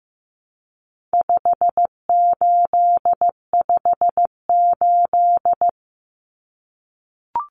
Morse code, heard as a keyed tone, about 15 wpm, 700 Hz. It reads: 5858
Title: Numbers - 4-Digit 15wpm